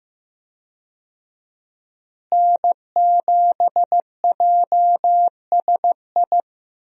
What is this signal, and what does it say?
Morse code, heard as a keyed tone, about 15 wpm, 700 Hz. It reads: N7JSI
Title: USA Call Signs - 5 Characters - Repeated 2x - 15wpm